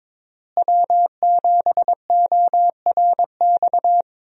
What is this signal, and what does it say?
Morse code, heard as a keyed tone, about 22 wpm, 700 Hz. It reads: W7ORX
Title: USA Call Signs - 5 Characters - Repeated 2x - 22wpm